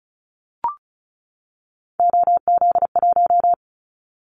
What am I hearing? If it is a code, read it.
Morse code, heard as a keyed tone, about 34 wpm, 700 Hz. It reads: OZ1
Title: Callsign Prefixes for Top 50 DXCC - 35wpm